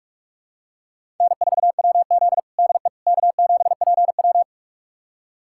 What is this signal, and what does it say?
Morse code, heard as a keyed tone, about 46 wpm, 700 Hz. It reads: N4WZ DE K7PW
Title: USA Callsign DE Callsign - 4 Characters - 45wpm